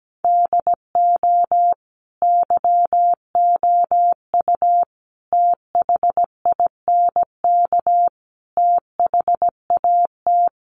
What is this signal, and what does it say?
Morse code, heard as a keyed tone, about 17 wpm, 700 Hz. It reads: DO YOU THINK THAT
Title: Sentences from Top 800 Words 17wpm